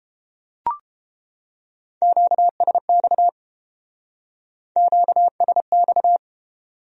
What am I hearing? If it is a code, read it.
Morse code, heard as a keyed tone, about 31 wpm, 700 Hz. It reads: QSX QSX